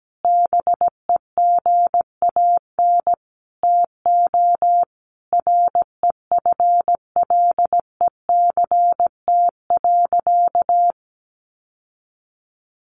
Morse code, heard as a keyed tone, about 17 wpm, 700 Hz. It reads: BEGAN TO REFLECT.
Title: Sentences from Top 1500 Words 17wpm